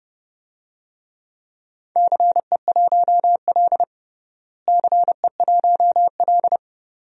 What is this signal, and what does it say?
Morse code, heard as a keyed tone, about 30 wpm, 700 Hz. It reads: CE1L CE1L
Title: DXCC-100 Call Signs - 4 Characters - Repeated 2x - 30wpm